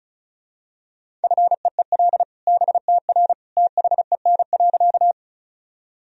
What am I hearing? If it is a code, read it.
Morse code, heard as a keyed tone, about 36 wpm, 700 Hz. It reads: FEEL BTR THEN.
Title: Sentences from Top 400 Words and Abbreviations 35wpm